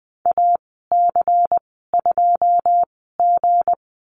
Morse code, heard as a keyed tone, about 20 wpm, spaced as for 14 wpm, 700 Hz. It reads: AC2G
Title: 161 Lesson 26g - Letter g - Call Signs 20wpm